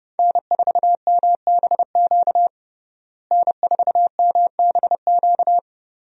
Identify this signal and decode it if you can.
Morse code, heard as a keyed tone, about 32 wpm, 700 Hz. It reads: N4MBQ N4MBQ
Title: USA Call Signs - 5 Characters - Repeated 3x - 30wpm